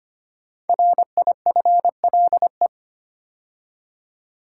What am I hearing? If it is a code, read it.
Morse code, heard as a keyed tone, about 25 wpm, 700 Hz. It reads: RIFLE
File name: 5 Letter Words - First 3 Letters Given - 25wpm